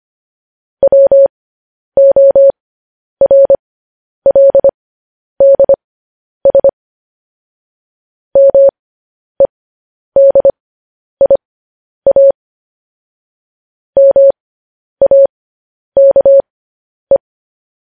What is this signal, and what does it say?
Morse code, heard as a keyed tone, about 25 wpm, spaced as for 10 wpm, 550 Hz. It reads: WORLDS MEDIA MAKE